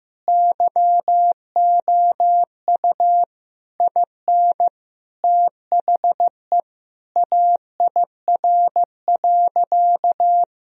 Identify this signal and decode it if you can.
Morse code, heard as a keyed tone, about 15 wpm, 700 Hz. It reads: YOU IN THE AIR.
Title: Sentences from Top 1000 Words - Review Part 1 - 15wpm